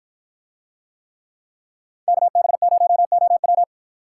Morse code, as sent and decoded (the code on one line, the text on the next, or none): -.- -... ----- --- .--
KB0OW